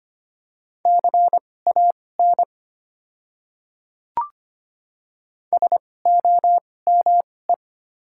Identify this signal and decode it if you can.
Morse code, heard as a keyed tone, about 25 wpm, spaced as for 18 wpm, 690 Hz. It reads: CAN SOME
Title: 2x Character Spacing - Top 300 Words - 25wpm